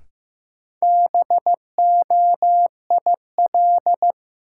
Morse code, heard as a keyed tone, about 15 wpm, 700 Hz. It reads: BOIL
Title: Sets of 5 Words Encore - New Word Drill - 15wpm